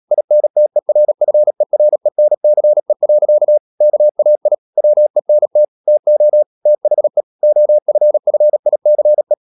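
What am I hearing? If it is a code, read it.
Morse code, heard as a keyed tone, about 37 wpm, 600 Hz. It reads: INTERFERENCE. KAI WENT TO THE OFFICE
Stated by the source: Morse code practice at 35–39 wpm sending English text of Ken Follett's novel Never, from CW med Gnister.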